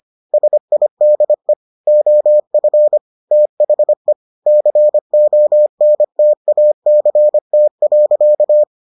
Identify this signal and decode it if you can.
Morse code, heard as a keyed tone, about 25 wpm, 600 Hz. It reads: SIDE OF THE CONTACT.